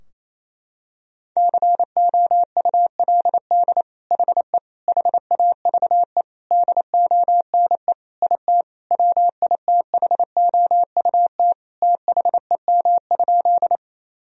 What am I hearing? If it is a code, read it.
Morse code, heard as a keyed tone, about 28 wpm, 700 Hz. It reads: COULD HE HAVE DONE IT WITHOUT THEM?
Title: Sentences from Top 1000 Words - Review Part 3 - 28wpm